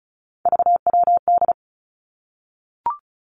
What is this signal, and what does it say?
Morse code, heard as a keyed tone, about 36 wpm, 700 Hz. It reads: VWD
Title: Three Challenging Letter-Number Combinations 35wpm